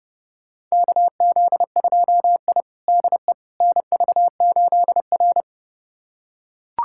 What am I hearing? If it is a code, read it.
Morse code, heard as a keyed tone, about 30 wpm, 700 Hz. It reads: KZ2I DE NV8R
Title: USA Callsign DE Callsign - 4 Characters - 30wpm